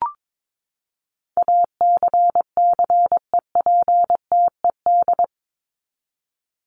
Morse code, heard as a keyed tone, about 22 wpm, 700 Hz. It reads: ACCEPTED
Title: ED Suffix - Top 100 Words with Base Words 22wpm